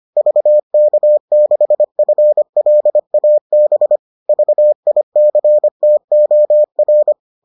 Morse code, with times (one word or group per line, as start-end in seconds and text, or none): 0.2-4.0 VK6FLAB
4.3-7.1 VICTOR